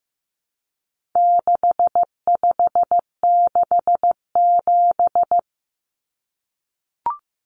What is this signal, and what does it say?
Morse code, heard as a keyed tone, about 15 wpm, 700 Hz. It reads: 6567
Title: Numbers - 4-Digit 15wpm